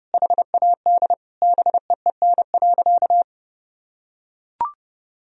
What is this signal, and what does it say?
Morse code, heard as a keyed tone, about 30 wpm, 700 Hz. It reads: HAD BEEN.
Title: Sentences from Top 1000 Words - Review Part 4 - 30wpm